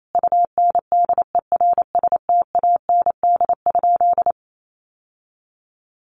Morse code, heard as a keyed tone, about 28 wpm, 700 Hz. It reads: UNDERSTAND?
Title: Sentences from Top 1500 Words 28wpm